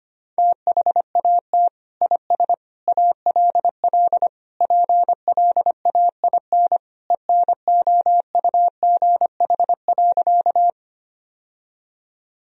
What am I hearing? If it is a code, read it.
Morse code, heard as a keyed tone, about 25 wpm, 700 Hz. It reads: THAT IS ALL PLAIN ENOUGH.
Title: Sentences from Top 1000 Words - Review Part 2 - 25wpm